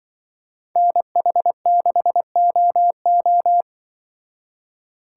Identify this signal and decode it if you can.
Morse code, heard as a keyed tone, about 22 wpm, 700 Hz. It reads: NH6OO
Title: Speed Racing - USA 2x2 Call Signs 20wpm